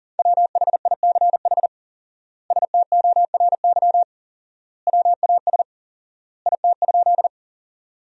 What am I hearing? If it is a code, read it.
Morse code, heard as a keyed tone, about 37 wpm, 700 Hz. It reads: WHICH STORY WAS IT?
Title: 4x Word-Spacing - Sentences from Top 300 Words - 40wpm